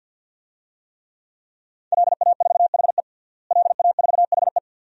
Morse code, heard as a keyed tone, about 52 wpm, 700 Hz. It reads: LA4HE LA4HE
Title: DXCC-100 Call Signs - 5 Characters - Repeated 3x - 50wpm